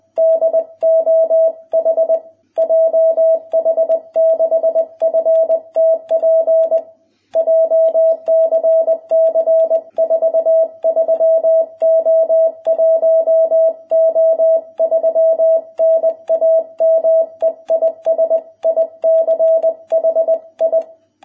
Morse code, heard as a keyed tone, about 20 wpm, 660 Hz. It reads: DOH JH6FTP JCC 43O1O3 NAME IS ICHI